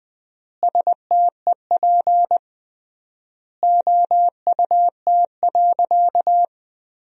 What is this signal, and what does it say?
Morse code, heard as a keyed tone, about 20 wpm, 700 Hz. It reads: STEP OUT.
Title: 3x Word-Spacing - Sentences from Top 400 Words - 20wpm